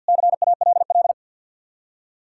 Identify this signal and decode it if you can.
Morse code, heard as a keyed tone, about 64 wpm, 700 Hz. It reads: CALL